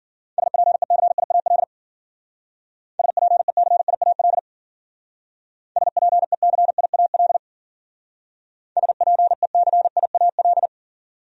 Morse code, heard as a keyed tone, about 48 wpm, 700 Hz. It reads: SPECIAL SPECIAL SPECIAL SPECIAL
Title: Speed-Racing Top 500 Words 40wpm